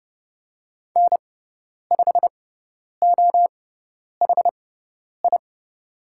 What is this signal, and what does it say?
Morse code, heard as a keyed tone, about 30 wpm, spaced as for 10 wpm, 700 Hz. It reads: N5OHI
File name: Lesson 16g - Letter c - Call Signs 30wpm